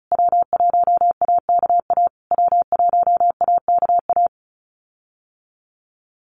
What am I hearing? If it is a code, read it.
Morse code, heard as a keyed tone, about 34 wpm, 700 Hz. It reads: W1AKA W1AKA
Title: USA Call Signs Repeated 3x 35wpm